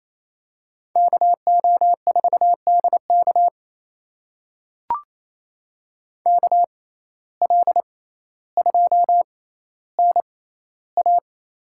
Morse code, heard as a keyed tone, about 28 wpm, spaced as for 14 wpm, 700 Hz. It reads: KO4DK KL2NA
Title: USA Call Signs - 5 Characters - 6x Character Spacing - 28wpm